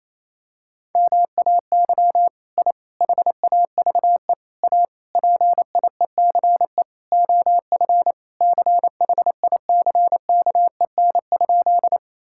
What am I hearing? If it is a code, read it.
Morse code, heard as a keyed tone, about 28 wpm, 700 Hz. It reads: MAY I HAVE A PIECE OF CHICKEN?
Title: Sentences from Top 1000 Words - Review Part 5 - 28wpm